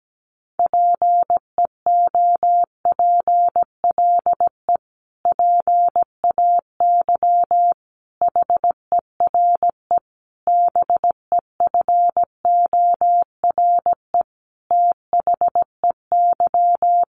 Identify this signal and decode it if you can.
Morse code, heard as a keyed tone, about 17 wpm, 700 Hz. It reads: PEOPLE PAY HERE BEFORE THEY
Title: Sentences from Top 700 Words 17wpm